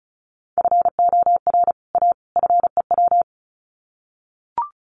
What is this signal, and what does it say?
Morse code, heard as a keyed tone, about 34 wpm, 700 Hz. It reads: FOR A FEW
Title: Sets of 3 Words 35wpm